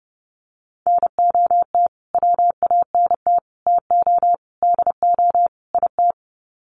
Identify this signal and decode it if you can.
Morse code, heard as a keyed tone, about 30 wpm, 700 Hz. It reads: NOT WANT TO DO IT
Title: Sets of 5 Words Encore 30wpm